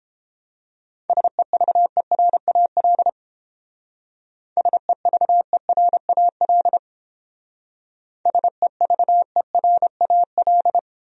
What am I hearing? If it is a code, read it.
Morse code, heard as a keyed tone, about 29 wpm, 700 Hz. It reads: SEVERAL SEVERAL SEVERAL